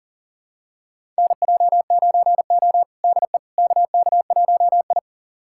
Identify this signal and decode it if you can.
Morse code, heard as a keyed tone, about 40 wpm, 700 Hz. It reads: NJ9O DE KK1I
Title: USA Callsign DE Callsign - 4 Characters - 40wpm